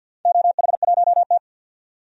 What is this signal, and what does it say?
Morse code, heard as a keyed tone, about 51 wpm, 700 Hz. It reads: OH1T
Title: DXCC-100 Call Signs - 4 Characters - 50wpm